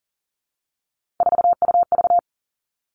Morse code, heard as a keyed tone, about 42 wpm, 700 Hz. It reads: 4UV
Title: Three Challenging Letter-Number Combinations 40wpm